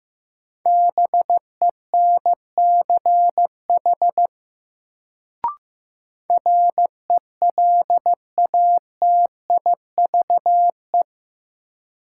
Morse code, heard as a keyed tone, about 15 wpm, 700 Hz. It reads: BENCH RELATIVE
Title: Sentences from Top 1400 Words - New Word Drill Alt - 15wpm